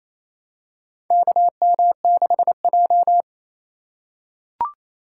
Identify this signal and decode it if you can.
Morse code, heard as a keyed tone, about 28 wpm, 700 Hz. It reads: KM6J
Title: USA Call Signs - 4x Character Spacing - 28wpm